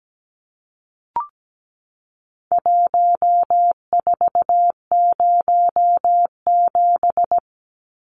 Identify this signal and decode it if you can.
Morse code, heard as a keyed tone, about 17 wpm, 700 Hz. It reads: 1407
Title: Numbers - Contest Style - 4-Digit 17wpm